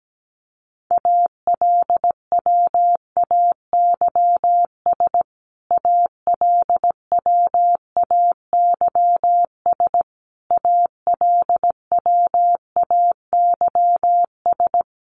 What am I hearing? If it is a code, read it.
Morse code, heard as a keyed tone, about 17 wpm, 700 Hz. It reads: ALWAYS ALWAYS ALWAYS
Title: Top 500 Words Repeated 17wpm